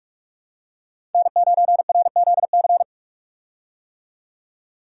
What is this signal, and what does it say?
Morse code, heard as a keyed tone, about 42 wpm, 700 Hz. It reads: N9RZC
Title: USA Call Signs - 5 Characters - 45wpm